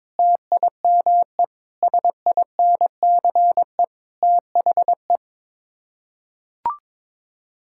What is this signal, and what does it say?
Morse code, heard as a keyed tone, about 22 wpm, 700 Hz. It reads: TIME SINCE THE
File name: practice / 22wpm / Sets of 6 Words - 22wpm